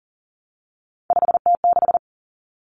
Morse code, heard as a keyed tone, about 40 wpm, 700 Hz. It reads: 5T6
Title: Common Cut Numbers - 3-Digit 40wpm